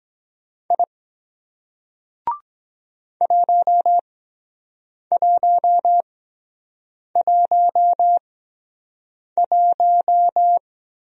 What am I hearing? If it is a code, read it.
Morse code, heard as a keyed tone, about 23 wpm, spaced as for 7 wpm, 700 Hz. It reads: I 1111